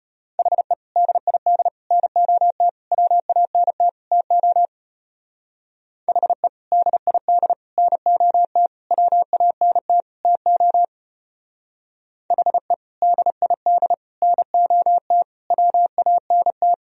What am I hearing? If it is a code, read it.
Morse code, heard as a keyed tone, about 31 wpm, 700 Hz. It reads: HE DID NOT WANT TO HE DID NOT WANT TO HE DID NOT WANT
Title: Speed-Racing - Sets of 5 Words 25wpm